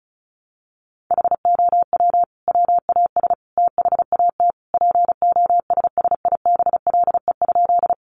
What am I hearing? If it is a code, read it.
Morse code, heard as a keyed tone, about 34 wpm, 700 Hz. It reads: HOW WAS THAT POSSIBLE?
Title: Sentences from Top 500 Words 35wpm